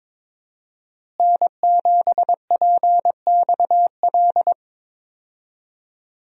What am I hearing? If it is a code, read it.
Morse code, heard as a keyed tone, about 22 wpm, 700 Hz. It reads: N7PXL
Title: USA Call Signs - 5 Characters - Repeated 3x - 22wpm